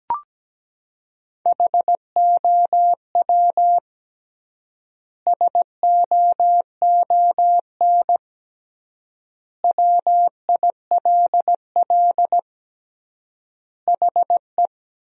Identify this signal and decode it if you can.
Morse code, heard as a keyed tone, about 17 wpm, 700 Hz. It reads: HOW SOON WILL HE
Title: 3x Word-Spacing - Sentences from Top 400 Words - 17wpm